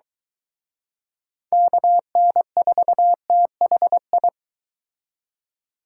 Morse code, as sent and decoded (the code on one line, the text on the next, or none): -.- -. ....- - .... ..
KN4THI